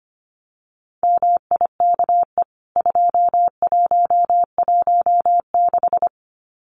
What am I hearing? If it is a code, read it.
Morse code, heard as a keyed tone, about 25 wpm, 700 Hz. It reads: MIKE 2116
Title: CWT Exchange 25wpm